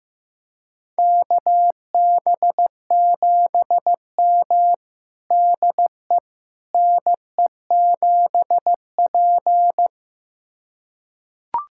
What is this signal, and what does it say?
Morse code, heard as a keyed tone, about 16 wpm, 700 Hz. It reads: KB7M DE NE7P
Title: USA Callsign DE Callsign - 4 Characters - 15wpm